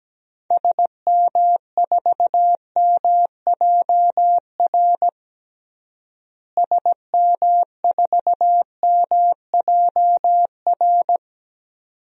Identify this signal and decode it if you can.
Morse code, heard as a keyed tone, about 17 wpm, 700 Hz. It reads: SM4MJR SM4MJR
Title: DXCC-100 Call Signs - 6 Characters - Repeated 2x - 17wpm